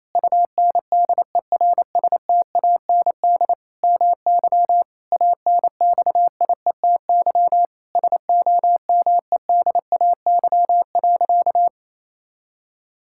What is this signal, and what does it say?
Morse code, heard as a keyed tone, about 28 wpm, 700 Hz. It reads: UNDERSTAND MY ANXIETY SOMEDAY.
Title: Sentences from Top 1600 Words 28wpm